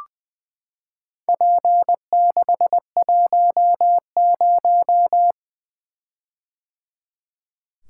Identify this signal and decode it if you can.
Morse code, heard as a keyed tone, about 21 wpm, 700 Hz. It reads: P610
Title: Four Letter-Number Combinations - 20wpm